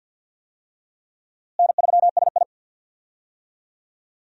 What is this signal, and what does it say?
Morse code, heard as a keyed tone, about 46 wpm, 700 Hz. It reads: N3SI
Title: USA Call Signs - 4 Characters - Repeated 2x - 50wpm